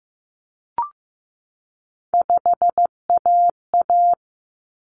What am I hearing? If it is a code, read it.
Morse code, heard as a keyed tone, about 15 wpm, 700 Hz. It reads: 5AA
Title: Common Cut Numbers - 3-Digit 15wpm